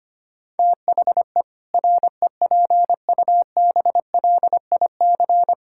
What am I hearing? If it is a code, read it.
Morse code, heard as a keyed tone, about 25 wpm, 700 Hz. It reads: THE REPUBLIC
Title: Sentences from Top 1600 Words 25wpm